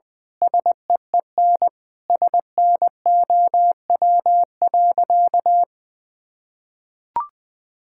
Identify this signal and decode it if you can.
Morse code, heard as a keyed tone, about 20 wpm, 700 Hz. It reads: SEEN SNOW.
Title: Sentences with 8-Word Phrases - Spoken Beginning - 20wpm